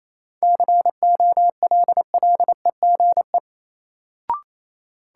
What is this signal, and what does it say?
Morse code, heard as a keyed tone, about 27 wpm, 700 Hz. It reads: COLLEGE
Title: Sentences from Top 1300 Words - New Word Drill Alt - 28wpm